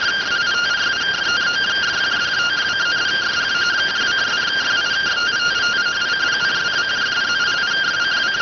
Two-tone teleprinter signal, about 170 Hz shift.